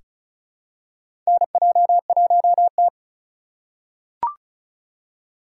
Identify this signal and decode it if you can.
Morse code, heard as a keyed tone, about 36 wpm, 700 Hz. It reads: NJ1T
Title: K1USN SST QSO Practice - 35wpm